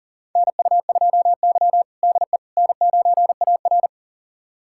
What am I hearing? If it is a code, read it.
Morse code, heard as a keyed tone, about 40 wpm, 700 Hz. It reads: NU2Y DE N9AR
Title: USA Callsign DE Callsign - 4 Characters - 40wpm